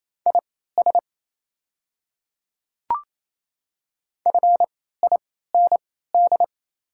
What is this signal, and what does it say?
Morse code, heard as a keyed tone, about 27 wpm, spaced as for 16 wpm, 690 Hz. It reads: IS FIND
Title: 3x Character Spacing - Top 100 Words - 28wpm